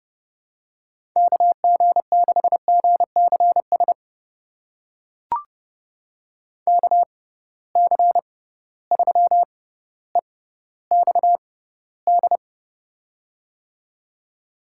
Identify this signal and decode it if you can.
Morse code, heard as a keyed tone, about 30 wpm, spaced as for 15 wpm, 700 Hz. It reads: KG6GCS KC3EXD
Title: USA Call Signs - 6 Characters - 6x Character Spacing - 30wpm